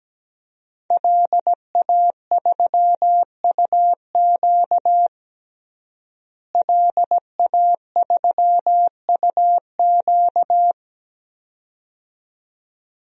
Morse code, heard as a keyed tone, about 17 wpm, 700 Hz. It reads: LA3UQ LA3UQ
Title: DXCC-100 Call Signs - 5 Characters - Repeated 2x - 17wpm